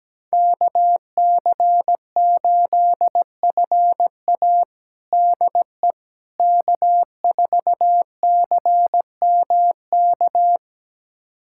Morse code, heard as a keyed tone, about 17 wpm, 700 Hz. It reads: KC8FA DE K4CMK
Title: USA Callsign DE Callsign - 5 Characters - 17wpm